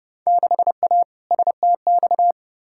Morse code, heard as a keyed tone, about 30 wpm, 700 Hz. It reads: 6A STX
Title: Field Day Exchanges - 30wpm